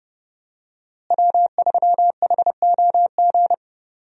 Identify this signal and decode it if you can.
Morse code, heard as a keyed tone, about 32 wpm, 700 Hz. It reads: W3HOG
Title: USA Call Signs - 5 Characters - Repeated 3x - 30wpm